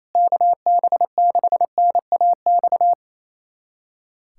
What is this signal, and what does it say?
Morse code, heard as a keyed tone, about 28 wpm, 700 Hz. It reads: KB6NAX
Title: USA Callsign DE Callsign - 6 Characters - 28wpm